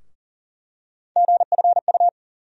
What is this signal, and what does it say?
Morse code, heard as a keyed tone, about 40 wpm, 700 Hz. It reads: GFU
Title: USA Call Sign Postfixes - 3 Letters 40wpm